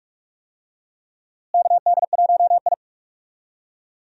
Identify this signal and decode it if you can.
Morse code, heard as a keyed tone, about 42 wpm, 700 Hz. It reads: KD1I
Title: USA Call Signs - 4 Characters - Repeated 3x - 45wpm